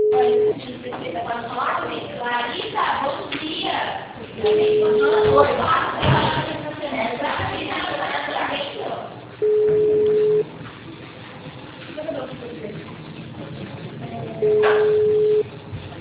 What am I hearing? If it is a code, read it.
Ringback tone (Europe, 425 Hz), with speech.